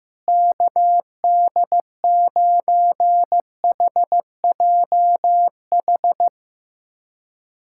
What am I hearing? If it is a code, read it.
Morse code, heard as a keyed tone, about 15 wpm, 700 Hz. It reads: KD9HJH
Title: USA Callsign DE Callsign - 6 Characters - 15wpm